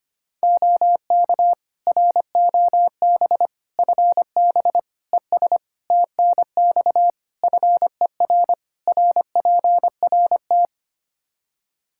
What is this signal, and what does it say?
Morse code, heard as a keyed tone, about 25 wpm, 700 Hz. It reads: OK ROB FB ES TNX FER RPRT
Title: Practice QSO - Exchange 2 - Info - 25wpm